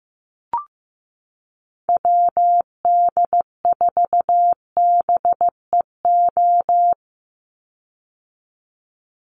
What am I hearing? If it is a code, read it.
Morse code, heard as a keyed tone, about 15 wpm, 700 Hz. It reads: WD4BEO
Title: USA Call Signs 15wpm